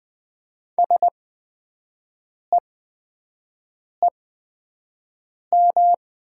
Morse code, heard as a keyed tone, about 20 wpm, spaced as for 5 wpm, 700 Hz. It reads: SEEM
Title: 8x Character Spacing - Top 300 Words - 20wpm